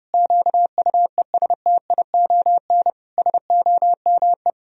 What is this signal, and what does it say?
Morse code, heard as a keyed tone, about 30 wpm, 700 Hz. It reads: QUESTION SOME
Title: Sentences from Top 1000 Words - Review Part 2 - 30wpm